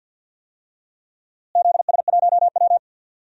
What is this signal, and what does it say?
Morse code, heard as a keyed tone, about 52 wpm, 700 Hz. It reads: ZS1W
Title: DXCC-100 Call Signs - 4 Characters - Repeated 2x - 50wpm